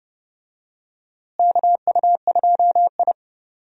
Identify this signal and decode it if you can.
Morse code, heard as a keyed tone, about 31 wpm, 700 Hz. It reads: KU2I
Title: USA Call Signs - 4 Characters - Repeated 2x - 30wpm